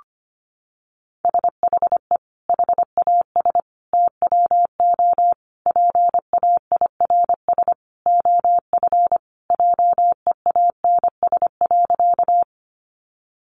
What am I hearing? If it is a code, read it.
Morse code, heard as a keyed tone, about 25 wpm, 700 Hz. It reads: SHE HAS TWO PAIRS OF JEANS.
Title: Additional Easy Phrases 25wpm